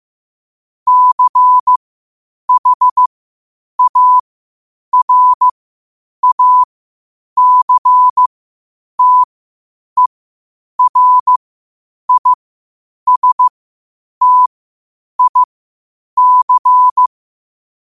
Morse code, heard as a keyed tone, about 15 wpm, spaced as for 8 wpm, 1000 Hz. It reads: CHARACTERISTIC
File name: qotdfiles / 8wpm